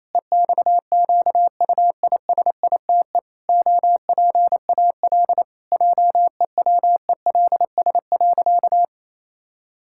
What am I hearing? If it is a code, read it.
Morse code, heard as a keyed tone, about 28 wpm, 700 Hz. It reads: EXQUISITE OPAL JEWELS.
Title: Pangram Challenge v2 - 28wpm